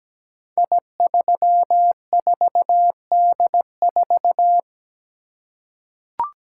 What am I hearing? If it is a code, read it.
Morse code, heard as a keyed tone, about 17 wpm, 700 Hz. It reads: I34D4
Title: Code Groups - 5 Letter:Numbers - 17wpm